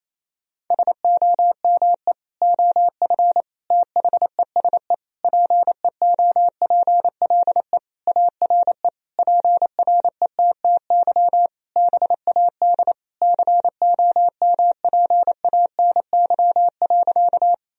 Morse code, heard as a keyed tone, about 28 wpm, 700 Hz. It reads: SOME OF THESE PEOPLE ARE PRETTY BAD COMPANY.
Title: Sentences from Top 1000 Words - Review Part 3 - 28wpm